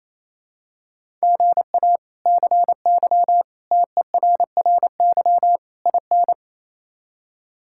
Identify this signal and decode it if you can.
Morse code, heard as a keyed tone, about 28 wpm, 700 Hz. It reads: GA CY TERRY IN
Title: K1USN SST QSO Practice - 28wpm